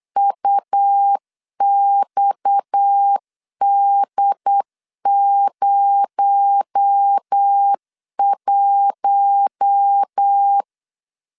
Morse code, heard as a keyed tone, about 8 wpm, 800 Hz. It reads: UXD01
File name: CW / DET / 1 6 1500C S C 8wpm 5219